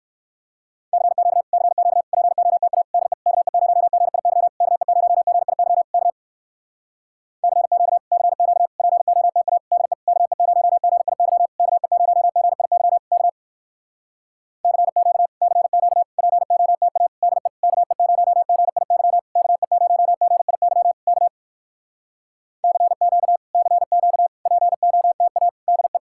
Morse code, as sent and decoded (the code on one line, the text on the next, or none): -.-. --.- / -.-. --.- / .--. --- - .- / -.. / -.- . ----- --. .. -.-- / -.- . ----- --. .. -.-- / -.- / -.-. --.- / -.-. --.- / .--. --- - .- / -.. / -.- . ----- --. .. -.-- / -.- . ----- --. .. -.-- / -.- / -.-. --.- / -.-. --.- / .--. --- - .- / -.. / -.- . ----- --. .. -.-- / -.- . ----- --. .. -.-- / -.- / -.-. --.- / -.-. --.- / .--. --- - .- / -.. .
CQ CQ POTA DE KE0GIY KE0GIY K CQ CQ POTA DE KE0GIY KE0GIY K CQ CQ POTA DE KE0GIY KE0GIY K CQ CQ POTA DE